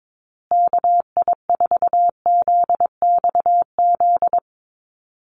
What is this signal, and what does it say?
Morse code, heard as a keyed tone, about 22 wpm, 700 Hz. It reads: KI4ZXZ
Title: USA Call Signs Repeated 2x 22wpm